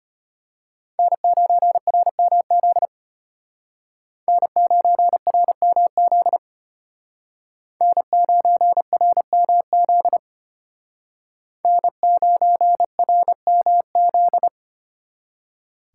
Morse code, heard as a keyed tone, about 33 wpm, 700 Hz. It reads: N9RMZ N9RMZ N9RMZ N9RMZ